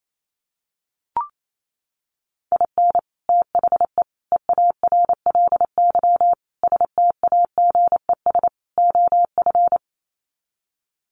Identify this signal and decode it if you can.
Morse code, heard as a keyed tone, about 28 wpm, 700 Hz. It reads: IN THE EARLY STAGES OF
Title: Sets of 5 Words Encore 28wpm